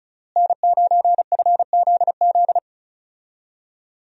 Morse code, heard as a keyed tone, about 36 wpm, 700 Hz. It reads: N9FZZ
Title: USA Call Signs - 5 Characters - Repeated 3x - 35wpm